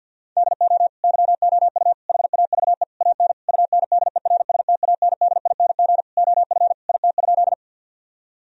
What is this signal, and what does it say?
Morse code, heard as a keyed tone, about 50 wpm, 700 Hz. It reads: DO YOU HAVE AN UNDERSTANDING OF IT?
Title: Sentences from Top 1000 Words - Review Part 3 - 50wpm